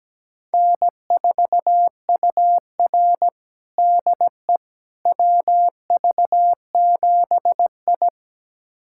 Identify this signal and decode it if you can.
Morse code, heard as a keyed tone, about 17 wpm, 700 Hz. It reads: N4UR DE WV7I
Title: USA Callsign DE Callsign - 4 Characters - 17wpm